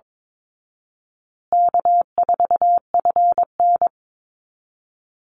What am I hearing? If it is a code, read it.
Morse code, heard as a keyed tone, about 22 wpm, 700 Hz. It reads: K4FN
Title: Call Signs Repeated 2x - 22wpm